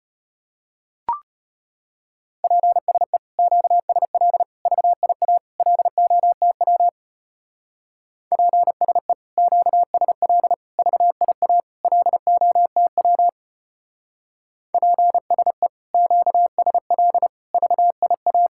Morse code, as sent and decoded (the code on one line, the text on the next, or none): .--. ... / --.- ... .-.. / ...- .. .- / .-.. --- - .-- / .--. ... / --.- ... .-.. / ...- .. .- / .-.. --- - .-- / .--. ... / --.- ... .-.. / ...- .. .-
PSE QSL VIA LOTW PSE QSL VIA LOTW PSE QSL VIA